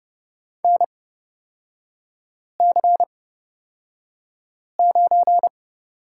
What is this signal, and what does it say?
Morse code, heard as a keyed tone, about 30 wpm, spaced as for 5 wpm, 700 Hz. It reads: NC9
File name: Lesson 37e - Letter x - Three Characters 30wpm